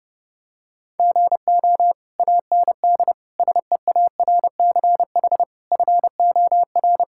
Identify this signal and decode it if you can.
Morse code, heard as a keyed tone, about 30 wpm, 700 Hz. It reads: GO AND SEARCH FOR
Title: Sentences from Top 1000 Words - Review Part 5 - 30wpm